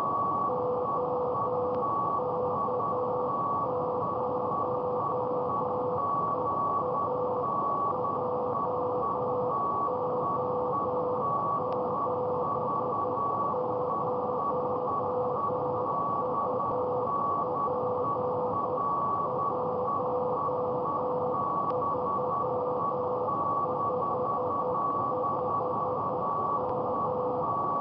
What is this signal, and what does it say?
Morse code, heard as a keyed tone, about 10 wpm, 500 Hz, with noise. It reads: OK0EA JO70UP 1355M
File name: mp3files / test ic7700 / beacon test 500 Hz